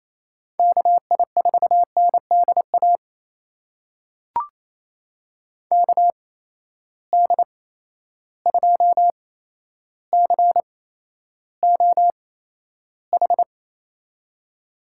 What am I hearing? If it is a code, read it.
Morse code, heard as a keyed tone, about 28 wpm, spaced as for 12 wpm, 700 Hz. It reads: KI4NDA KD2COH
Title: USA Call Signs - 6 Characters - 8x Character Spacing - 28wpm